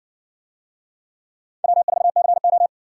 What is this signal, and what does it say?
Morse code, heard as a keyed tone, about 60 wpm, 700 Hz. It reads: W4CO